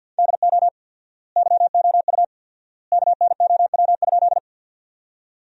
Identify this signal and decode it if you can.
Morse code, heard as a keyed tone, about 52 wpm, 700 Hz. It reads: DO YOU KNOW?
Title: 4x Word-Spacing - Sentences from Top 100 Words - 50wpm